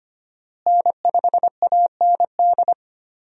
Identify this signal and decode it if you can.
Morse code, heard as a keyed tone, about 25 wpm, 700 Hz. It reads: N5AND
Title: USA Call Signs - 5 Characters - Repeated 2x - 25wpm